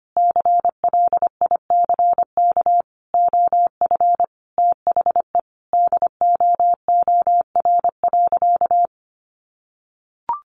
Morse code, heard as a keyed tone, about 25 wpm, 700 Hz. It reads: CLICK OF THE DOOR.
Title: Sentences from Top 1100 Words 25wpm